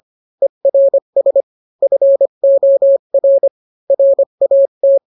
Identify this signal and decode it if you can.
Morse code, heard as a keyed tone, about 25 wpm, 550 Hz. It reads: ERS FOR RAT